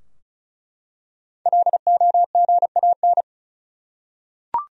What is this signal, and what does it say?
Morse code, heard as a keyed tone, about 34 wpm, 700 Hz. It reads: LOGAN
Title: USA Common Names - 35wpm